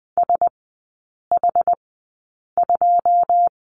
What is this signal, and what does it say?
Morse code, heard as a keyed tone, about 20 wpm, spaced as for 8 wpm, 700 Hz. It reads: SH2
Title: 093 Lesson 16e - Letter c - Three Characters 20wpm